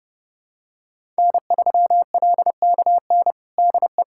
Morse code, heard as a keyed tone, about 30 wpm, 700 Hz. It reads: N3LKN DE
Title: USA Callsign DE Callsign 30wpm